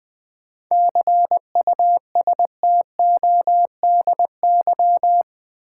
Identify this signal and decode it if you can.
Morse code, heard as a keyed tone, about 20 wpm, 700 Hz. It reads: CUSTODY
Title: 7 Letter Words - First 3 Letters Given - 20wpm